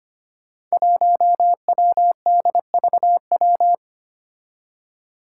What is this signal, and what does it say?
Morse code, heard as a keyed tone, about 25 wpm, 700 Hz. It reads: 1WDVW
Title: Five Challenging Letter-Number Combinations 25wpm